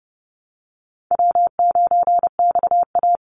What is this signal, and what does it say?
Morse code, heard as a keyed tone, about 30 wpm, 700 Hz. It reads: W9XA
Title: Call Signs Repeated 3x - 30wpm